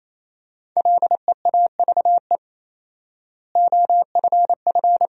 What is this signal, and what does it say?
Morse code, heard as a keyed tone, about 28 wpm, 700 Hz. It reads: LEAVE OFF
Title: 4x Word-Spacing - Sentences from Top 400 Words - 28wpm